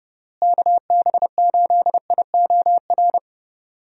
Morse code, heard as a keyed tone, about 30 wpm, 700 Hz. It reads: KB8IOR
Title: USA Call Signs - 6 Characters - Repeated 3x - 30wpm